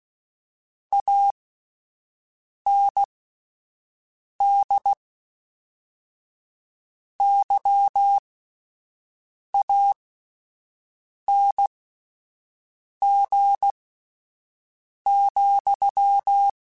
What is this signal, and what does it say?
Morse code, heard as a keyed tone, about 16 wpm, spaced as for 5 wpm, 800 Hz. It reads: AND YANG,
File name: Misc / Misc-mp3s / Word Pairs @5